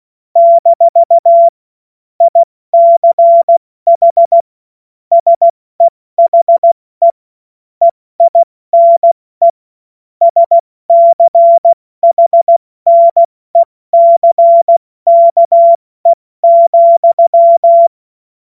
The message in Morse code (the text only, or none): - ICH SEHE EINE SCHNECKE,